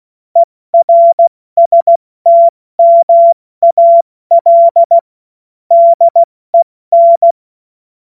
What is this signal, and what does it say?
Morse code, heard as a keyed tone, about 16 wpm, 680 Hz. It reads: ERSTMAL DEN
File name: Morsekurs / Witze / Witz 26. Oktober 2021, 70 F80